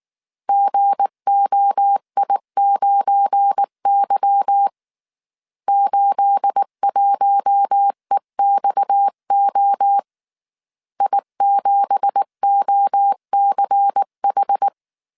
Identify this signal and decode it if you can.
Morse code, heard as a keyed tone, about 19 wpm, 800 Hz. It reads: GOI9Y 81EXO I7OCH
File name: CW / DET / 1 9 1500C S B 18wpm 2551